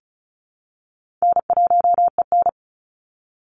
Morse code, heard as a keyed tone, about 34 wpm, 700 Hz. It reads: N1EN
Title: Call Signs Repeated 2x - 35wpm